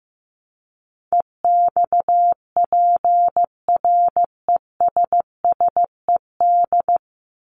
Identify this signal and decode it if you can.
Morse code, heard as a keyed tone, about 15 wpm, 700 Hz. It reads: EXPRESSED
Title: ED Suffix - Top 100 Words with Base Words 15wpm